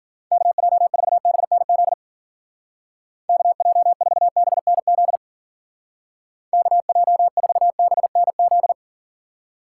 Morse code, heard as a keyed tone, about 48 wpm, 700 Hz. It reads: KJ4BNZ KJ4BNZ KJ4BNZ
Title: Speed Racing - USA 2x3 Call Signs 40wpm